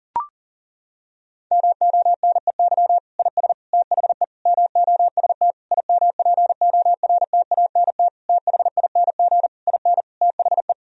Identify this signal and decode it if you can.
Morse code, heard as a keyed tone, about 40 wpm, 700 Hz. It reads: MONEY IS THE MOST IMPORTANT THING IN THE
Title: Sentences from Top 1000 Words - Review Part 1 - 40wpm